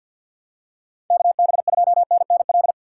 Morse code, heard as a keyed tone, about 46 wpm, 690 Hz. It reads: KB2NNL
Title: USA Call Signs - 6 Characters - 3x Character Spacing - 50wpm